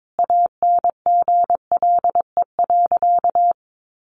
Morse code, heard as a keyed tone, about 22 wpm, 700 Hz. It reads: ANGLE.
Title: Sentences from Top 1500 Words 22wpm